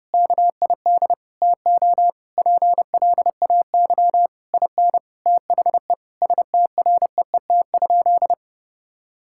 Morse code, heard as a keyed tone, about 30 wpm, 700 Hz. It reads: KID TO PLAY IN THE STREET?
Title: Sentences from Top 1600 Words 30wpm